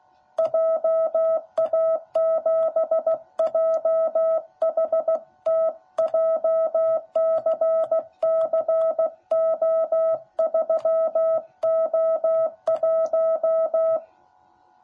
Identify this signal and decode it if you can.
Morse code, heard as a keyed tone, about 15 wpm, 660 Hz. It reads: JA7JHT JCCO3O1
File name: jh8cbh / 2021cw / cw10b